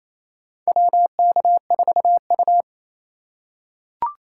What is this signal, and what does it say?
Morse code, heard as a keyed tone, about 27 wpm, 690 Hz. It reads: WK4U
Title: USA Call Signs - 4 Characters - 2x Character Spacing - 28wpm